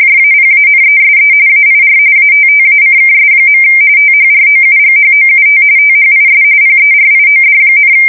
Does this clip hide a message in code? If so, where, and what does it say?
Yes: RTTY from 0.0 to 7.9 s, about 45.45 baud. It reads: JAPANESE AIR ATTACKS. THROUGHOUT THIS PERIOD,